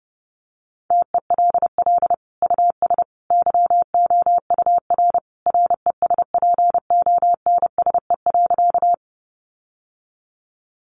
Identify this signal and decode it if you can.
Morse code, heard as a keyed tone, about 30 wpm, 700 Hz. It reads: TELL US YOUR RESPONSE.
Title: Sentences from Top 1400 Words 30wpm